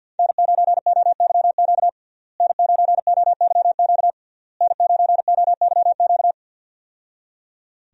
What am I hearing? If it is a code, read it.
Morse code, heard as a keyed tone, about 52 wpm, 700 Hz. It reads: N9OYQ N9OYQ N9OYQ
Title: USA Call Signs - 5 Characters - Repeated 3x - 50wpm